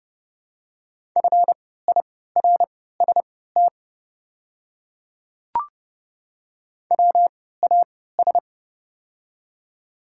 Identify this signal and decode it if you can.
Morse code, heard as a keyed tone, about 30 wpm, spaced as for 17 wpm, 700 Hz. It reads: FIRST WAS
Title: 3x Character Spacing - Top 100 Words - 30wpm